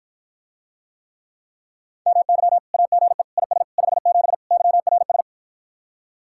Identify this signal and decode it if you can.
Morse code, heard as a keyed tone, about 52 wpm, 700 Hz. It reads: MY AGE IS 57 YRS